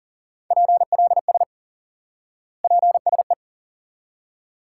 Morse code, heard as a keyed tone, about 40 wpm, 700 Hz. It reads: PLS PSE
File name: IQE Course 3 - Lesson 89 - YAGI - Two Elements - 40wpm